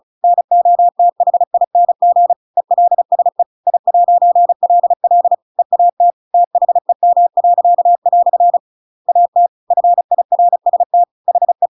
Morse code, heard as a keyed tone, about 34 wpm, 700 Hz. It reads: NOTHING ELSE I'LL EAT THEM." AT FIRST HE
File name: Pinocchio / Pinocchio ch7 35 35 ik1hge